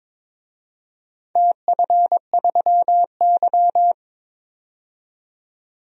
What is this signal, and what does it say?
Morse code, heard as a keyed tone, about 23 wpm, 700 Hz. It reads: TF3Y
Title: DXCC-100 Call Signs - 4 Characters - 22wpm